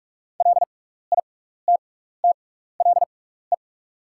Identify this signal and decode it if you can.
Morse code, heard as a keyed tone, about 43 wpm, spaced as for 15 wpm, 690 Hz. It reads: LITTLE
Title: 6x Character Spacing - Top 200 Words - 45wpm